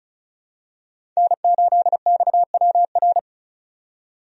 Morse code, heard as a keyed tone, about 34 wpm, 700 Hz. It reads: N8XWR
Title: USA Call Signs - 5 Characters - 35wpm